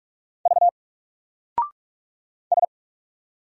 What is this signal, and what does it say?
Morse code, heard as a keyed tone, about 44 wpm, 700 Hz. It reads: V S
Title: Single Letter-Number Alt 45wpm